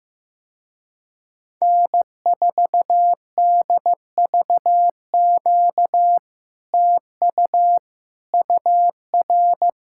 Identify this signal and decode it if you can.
Morse code, heard as a keyed tone, about 15 wpm, 700 Hz. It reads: N4DVQ TU UR
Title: Speed-Racing Common QSO Phrases Encore - 15wpm